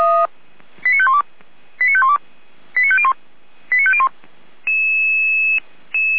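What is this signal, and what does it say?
Five-tone selective call (ZVEI), heard as quick runs of five tones, tones of about 70 ms. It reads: 89512 89512 89501 89501
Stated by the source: Five-tone ZVEI selective call received on an SDR.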